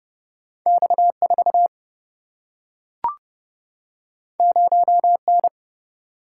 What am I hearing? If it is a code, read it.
Morse code, heard as a keyed tone, about 30 wpm, 700 Hz. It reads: X4 0N
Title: Single-Letter Single-Number 30wpm